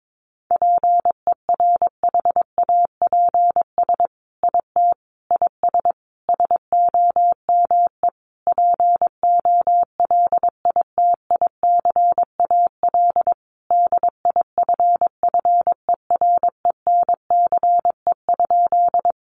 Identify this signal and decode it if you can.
Morse code, heard as a keyed tone, about 22 wpm, 700 Hz. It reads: PERHAPS IT IS SOME POLITICAL DIFFERENCE?
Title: Sentences from Top 800 Words 22wpm